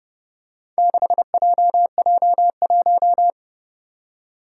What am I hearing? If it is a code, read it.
Morse code, heard as a keyed tone, about 29 wpm, 700 Hz. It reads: 6JJ1
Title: Four Challenging Letter-Number Combinations 30wpm